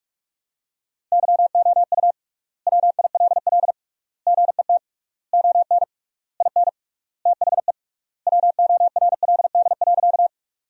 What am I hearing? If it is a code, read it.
Morse code, heard as a keyed tone, about 46 wpm, 700 Hz. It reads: YOU WILL GET ON IN THE WORLD.
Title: 3x Word-Spacing - Sentences from Top 200 Words - 45wpm